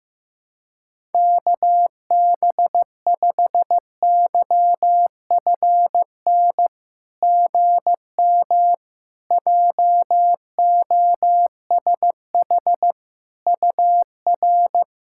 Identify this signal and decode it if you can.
Morse code, heard as a keyed tone, about 15 wpm, 700 Hz. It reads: KB5YFN GM JOSH UR
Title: SOTA QSO Practice 15wpm